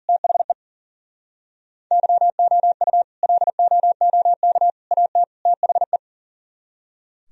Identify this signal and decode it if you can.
Morse code, heard as a keyed tone, about 52 wpm, 700 Hz. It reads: THE YOU LOOK AT THE